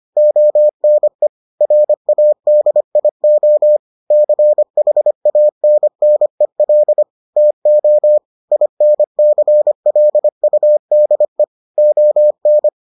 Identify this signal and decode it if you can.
Morse code, heard as a keyed tone, about 25 wpm, 600 Hz. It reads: ONE RADIO CHANNEL TO INCLUDE ON